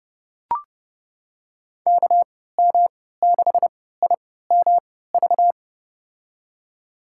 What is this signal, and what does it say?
Morse code, heard as a keyed tone, about 30 wpm, spaced as for 17 wpm, 700 Hz. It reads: KM6IMV
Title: USA Call Signs - 6 Characters - 3x Character Spacing - 30wpm